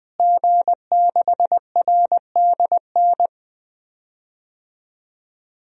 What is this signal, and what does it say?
Morse code, heard as a keyed tone, about 20 wpm, 700 Hz. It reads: G6RDN
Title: DXCC-100 Call Signs - 5 Characters - Repeated 2x - 20wpm